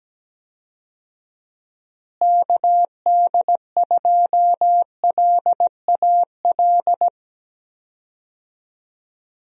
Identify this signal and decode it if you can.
Morse code, heard as a keyed tone, about 17 wpm, 700 Hz. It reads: KD2LAL
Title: USA Call Signs - 6 Characters - Repeated 2x - 17wpm